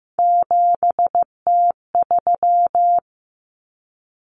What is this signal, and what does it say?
Morse code, heard as a keyed tone, about 15 wpm, 700 Hz. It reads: 7T3
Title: Common Cut Numbers - 3-Digit 15wpm